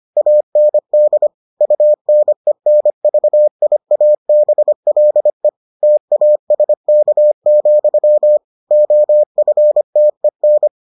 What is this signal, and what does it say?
Morse code, heard as a keyed tone, about 25 wpm, 600 Hz. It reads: AND UNENVIABLE TASK, OFTEN